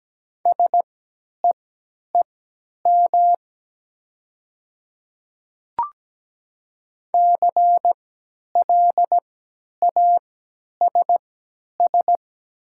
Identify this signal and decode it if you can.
Morse code, heard as a keyed tone, about 17 wpm, spaced as for 10 wpm, 700 Hz. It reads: SEEM CLASS
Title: 3x Character Spacing - Top 500 Words - 17wpm